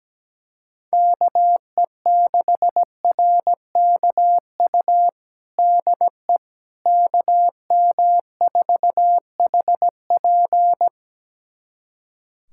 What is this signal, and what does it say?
Morse code, heard as a keyed tone, about 17 wpm, 700 Hz. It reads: KE6RKU DE KM4HP
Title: USA Callsign DE Callsign 17wpm